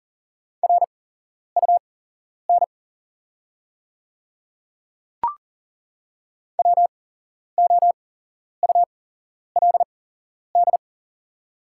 Morse code, heard as a keyed tone, about 40 wpm, spaced as for 11 wpm, 690 Hz. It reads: RUN WOULD
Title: 8x Character Spacing - Top 300 Words - 40wpm